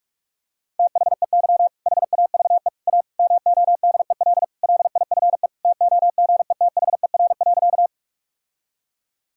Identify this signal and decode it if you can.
Morse code, heard as a keyed tone, about 46 wpm, 700 Hz. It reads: THEY HAVE A MODEL LIFE TOGETHER.
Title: Sentences from Top 1000 Words - Review Part 5 - 45wpm